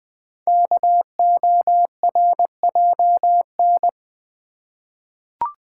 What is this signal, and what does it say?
Morse code, heard as a keyed tone, about 20 wpm, 700 Hz. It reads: KORJN
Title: Code Groups - 5 Letters - 20wpm